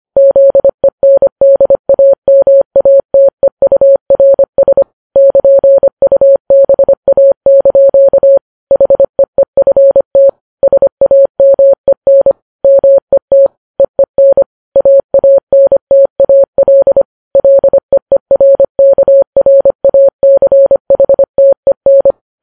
Morse code, heard as a keyed tone, about 25 wpm, 550 Hz. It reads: ZENDAMATEURS <KN>UBA) HEEFT SAMEN MET EEN AANTAL LEERKRACHTEN
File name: Files Web tls / Morse on9fv / morse25kps